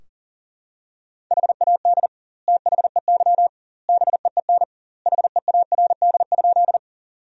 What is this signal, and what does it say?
Morse code, heard as a keyed tone, about 40 wpm, 700 Hz. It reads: HAD THEY BEEN HEARD?
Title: 2x Word-Spacing - Sentences from Top 400 Words - 40wpm